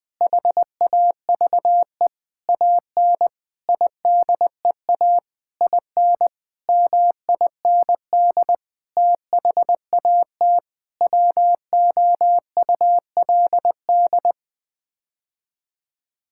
Morse code, heard as a keyed tone, about 20 wpm, 700 Hz. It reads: HAVE AN IDEA IN MIND THAT WOULD
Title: Sets of 9 Words - 20wpm